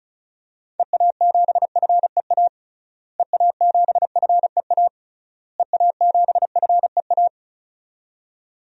Morse code, heard as a keyed tone, about 36 wpm, 700 Hz. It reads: EA7FEA EA7FEA EA7FEA
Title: DXCC-100 Call Signs - 6 Characters - Repeated 3x - 35wpm